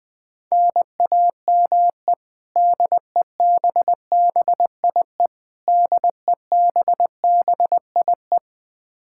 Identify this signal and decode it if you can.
Morse code, heard as a keyed tone, about 20 wpm, 700 Hz. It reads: NAME DEBBIE DEBBIE
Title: Practice QSO - Exchange 1 - Beginning - 20wpm